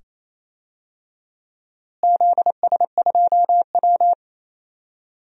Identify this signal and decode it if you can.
Morse code, heard as a keyed tone, about 28 wpm, 700 Hz. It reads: ZS2W
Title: DXCC-100 Call Signs - 4 Characters - Repeated 2x - 28wpm